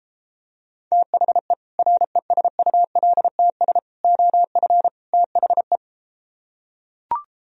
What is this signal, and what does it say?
Morse code, heard as a keyed tone, about 33 wpm, 700 Hz. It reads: THE RESULTS OF THE